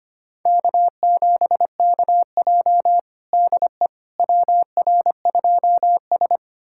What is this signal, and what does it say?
Morse code, heard as a keyed tone, about 25 wpm, 700 Hz. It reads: K7KJ DE WR2S
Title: USA Callsign DE Callsign - 4 Characters - 25wpm